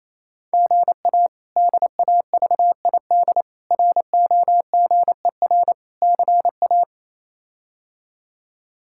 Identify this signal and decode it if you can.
Morse code, heard as a keyed tone, about 28 wpm, 700 Hz. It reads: GA DAVID ROGER CA
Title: K1USN SST QSO Practice - 28wpm